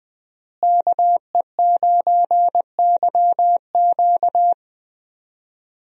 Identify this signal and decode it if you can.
Morse code, heard as a keyed tone, about 20 wpm, 700 Hz. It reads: KE9YQ
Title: USA Call Signs - 5 Characters - Repeated 2x - 20wpm